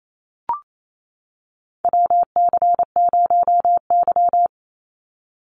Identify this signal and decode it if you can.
Morse code, heard as a keyed tone, about 28 wpm, 700 Hz. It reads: WC0Y
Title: NA SOTA Chaser Callsigns 28wpm